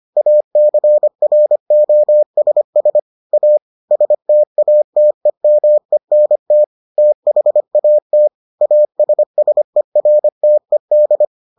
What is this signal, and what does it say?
Morse code, heard as a keyed tone, about 25 wpm, 600 Hz. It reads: ACROSS A STATEMENT THAT ASSERTED